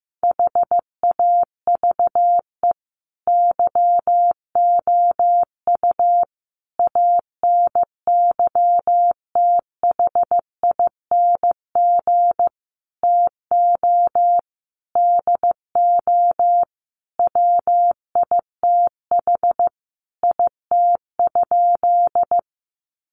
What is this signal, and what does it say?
Morse code, heard as a keyed tone, about 15 wpm, 700 Hz. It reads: HAVE YOU ANYTHING TO DO WITH IT?
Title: Sentences from Top 1000 Words 15wpm